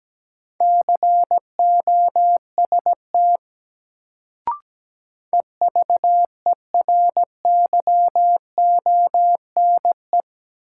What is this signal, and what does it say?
Morse code, heard as a keyed tone, about 17 wpm, 700 Hz. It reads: COST EVERYONE
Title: Sentences from Top 800 Words - New Word Drill - 17wpm